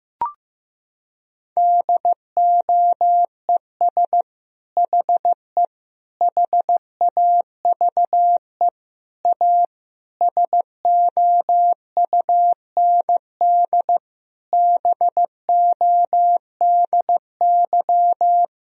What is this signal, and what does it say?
Morse code, heard as a keyed tone, about 15 wpm, 700 Hz. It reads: DOES HE HAVE A SOUND BODY
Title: Sentences from Top 1000 Words - Review Part 2 - 15wpm